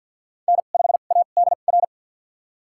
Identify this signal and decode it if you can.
Morse code, heard as a keyed tone, about 50 wpm, spaced as for 36 wpm, 700 Hz. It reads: N5ADR
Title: USA Call Signs - 2x Character Spacing - 50wpm